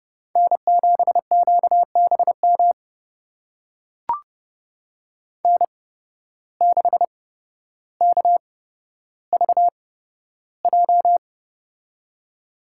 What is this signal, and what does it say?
Morse code, heard as a keyed tone, about 31 wpm, spaced as for 13 wpm, 700 Hz. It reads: N7QBM N6KVJ
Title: USA Call Signs - 5 Characters - 8x Character Spacing - 30wpm